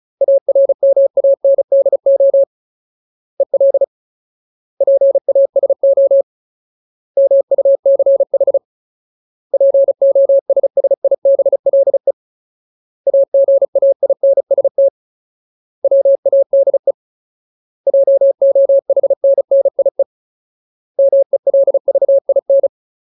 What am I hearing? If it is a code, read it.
Morse code, heard as a keyed tone, about 34 wpm, 550 Hz. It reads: ARMANDO EL PASO MUCH POSSIBLE AGAINST WADE JOHNNIE MELVIN
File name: Ultimate Word List / 35WPM Ultimate